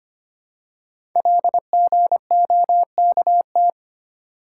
Morse code, heard as a keyed tone, about 25 wpm, 700 Hz. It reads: LGOKT
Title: Code Groups - 5 Letters - 25wpm